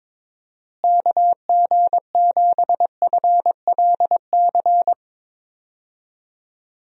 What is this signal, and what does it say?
Morse code, heard as a keyed tone, about 22 wpm, 700 Hz. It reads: KG7FLC
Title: USA Call Signs - 6 Characters - Repeated 2x - 22wpm